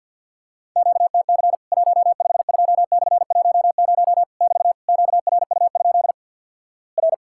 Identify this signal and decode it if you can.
Morse code, heard as a keyed tone, about 50 wpm, 700 Hz. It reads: QTC 152/10 <BT> QRU? R
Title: Worked All Europe DX Contest - QTC Exchange - 50wpm